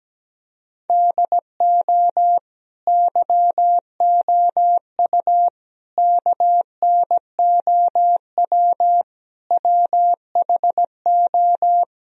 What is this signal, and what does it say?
Morse code, heard as a keyed tone, about 17 wpm, 700 Hz. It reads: DO YOU KNOW WHO
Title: Sentences from Top 1000 Words - Review Part 2 - 17wpm